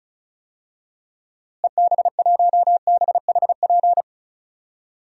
Morse code, heard as a keyed tone, about 36 wpm, 700 Hz. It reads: EB1BHP
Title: DXCC-100 Call Signs - 6 Characters - Repeated 2x - 35wpm